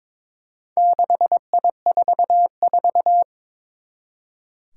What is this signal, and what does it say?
Morse code, heard as a keyed tone, about 22 wpm, 700 Hz. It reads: 6I44
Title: Four Challenging Letter-Number Combinations 22wpm